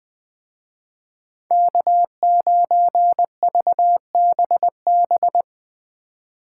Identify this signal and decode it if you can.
Morse code, heard as a keyed tone, about 20 wpm, 700 Hz. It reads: K9VBB
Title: Speed Racing - USA 1x3 Call Signs 17wpm